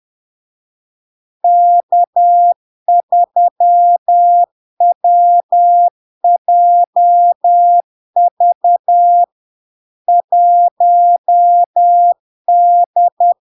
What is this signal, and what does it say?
Morse code, heard as a keyed tone, about 10 wpm, 700 Hz. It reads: K3WJV 1D
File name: LARK-FD-2022-Log-in-CW-10wpm-700-Hz